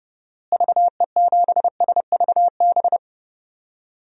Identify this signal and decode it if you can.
Morse code, heard as a keyed tone, about 32 wpm, 700 Hz. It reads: VE7SVB
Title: DXCC-100 Call Signs - 6 Characters - Repeated 2x - 30wpm